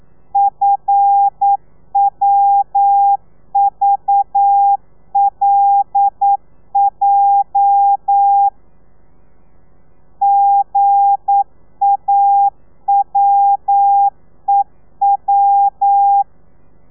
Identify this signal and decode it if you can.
Morse code, heard as a keyed tone, about 9 wpm, 800 Hz. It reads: FWVLJ GAWEW
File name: telegrafia / i2tfj / Lez8 40